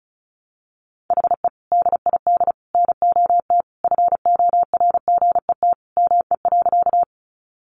Morse code, heard as a keyed tone, about 34 wpm, 700 Hz. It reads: HE DID NOT FORGET ME.
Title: Sentences from Top 1200 Words 35wpm